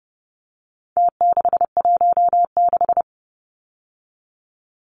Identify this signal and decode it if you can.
Morse code, heard as a keyed tone, about 30 wpm, 700 Hz. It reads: T616
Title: Numbers - Contest Style - 4-Digit 30wpm